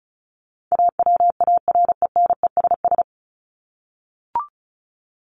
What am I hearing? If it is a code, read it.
Morse code, heard as a keyed tone, about 34 wpm, 700 Hz. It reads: AWARENESS
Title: Top 1501-1750 Words 35wpm